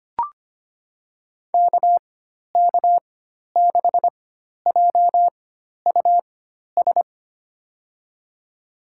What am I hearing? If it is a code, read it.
Morse code, heard as a keyed tone, about 25 wpm, spaced as for 12 wpm, 700 Hz. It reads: KK6JUS
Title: USA Call Signs - 4x Character Spacing - 25wpm